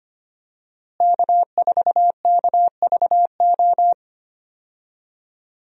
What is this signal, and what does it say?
Morse code, heard as a keyed tone, about 25 wpm, 700 Hz. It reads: K4KVO
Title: USA Call Signs - 5 Characters - Repeated 3x - 25wpm